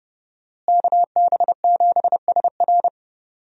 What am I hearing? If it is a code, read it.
Morse code, heard as a keyed tone, about 32 wpm, 700 Hz. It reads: KB7SR
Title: USA Call Signs - 5 Characters - Repeated 3x - 30wpm